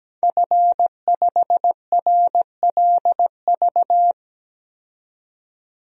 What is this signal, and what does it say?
Morse code, heard as a keyed tone, about 17 wpm, 700 Hz. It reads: F5RLV
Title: DXCC-100 Call Signs - 5 Characters - 17wpm